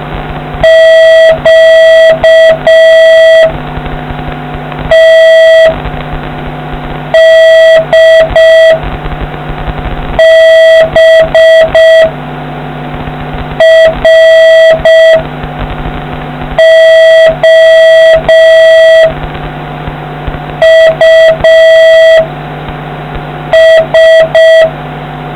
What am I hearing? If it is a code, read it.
Morse code, heard as a keyed tone, about 6 wpm, 3250 Hz, with noise. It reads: Q T D B R O U S